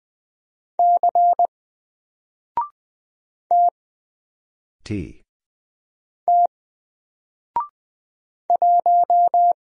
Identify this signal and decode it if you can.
Morse code, heard as a keyed tone, about 20 wpm, 700 Hz. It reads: C T T 1
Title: Single Character (Letter:Number:Punctuation:Prosign) Alt 20wpm